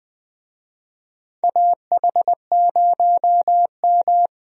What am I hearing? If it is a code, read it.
Morse code, heard as a keyed tone, about 20 wpm, 700 Hz. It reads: AH0M
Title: USA Call Signs - 4 Characters - 20wpm